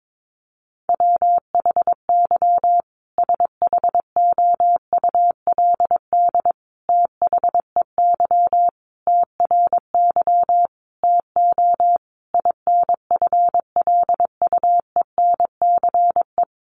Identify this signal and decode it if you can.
Morse code, heard as a keyed tone, about 22 wpm, 700 Hz. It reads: WHY SHOULD THEY TRY TO INFLUENCE
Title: Sentences from Top 1100 Words 22wpm